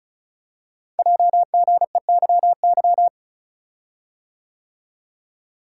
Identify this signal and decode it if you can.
Morse code, heard as a keyed tone, about 36 wpm, 690 Hz. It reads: JGEYY
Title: Code Groups - 5 Letters - 35wpm